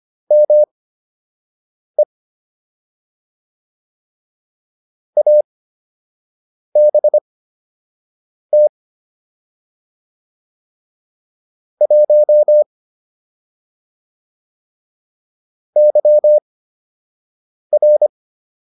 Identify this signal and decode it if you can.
Morse code, heard as a keyed tone, about 25 wpm, spaced as for 6 wpm, 600 Hz. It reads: ME ABT 1 YR